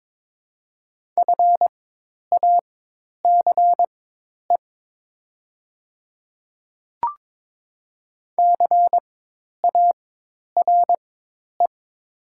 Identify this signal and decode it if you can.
Morse code, heard as a keyed tone, about 22 wpm, spaced as for 10 wpm, 700 Hz. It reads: FACE CARE
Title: 4x Character Spacing - Top 300 Words - 22wpm